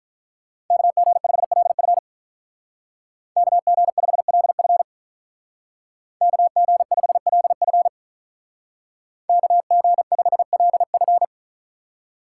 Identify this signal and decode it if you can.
Morse code, heard as a keyed tone, about 44 wpm, 700 Hz. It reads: KG5LF KG5LF KG5LF KG5LF